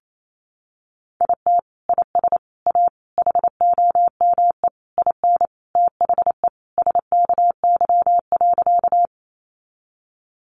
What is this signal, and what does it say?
Morse code, heard as a keyed tone, about 28 wpm, 700 Hz. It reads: IT IS A HOME IN THE SKY.
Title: Sentences from Top 1200 Words 28wpm